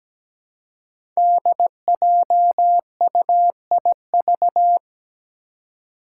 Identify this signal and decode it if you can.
Morse code, heard as a keyed tone, about 17 wpm, 700 Hz. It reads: DJUIV
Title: Five Challenging Letter-Number Combinations 17wpm